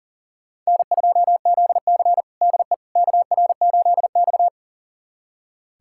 Morse code, heard as a keyed tone, about 40 wpm, 700 Hz. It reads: N2ZC DE KR8X
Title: USA Callsign DE Callsign - 4 Characters - 40wpm